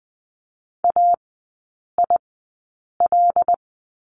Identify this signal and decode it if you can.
Morse code, heard as a keyed tone, about 20 wpm, spaced as for 8 wpm, 690 Hz. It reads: AIL
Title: 166 Lesson 27e - Number 7 - Three Characters 20wpm